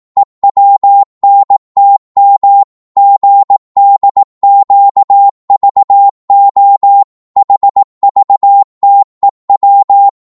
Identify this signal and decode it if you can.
Morse code, heard as a keyed tone, about 18 wpm, 800 Hz. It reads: EWNTM GDQVO HVTEW
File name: Morse / ltr-18wpm